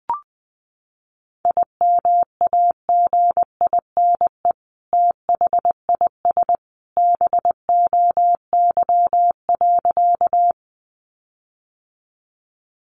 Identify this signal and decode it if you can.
Morse code, heard as a keyed tone, about 20 wpm, 700 Hz. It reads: IMAGINE THIS BOY.
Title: Sentences from Top 1400 Words 20wpm